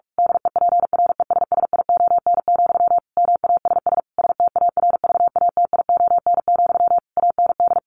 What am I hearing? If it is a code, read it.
Morse code, heard as a keyed tone, about 44 wpm, 700 Hz. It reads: DEPRESSION, MASS STARVATION, AND
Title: A Sound Of Thunder 45wpm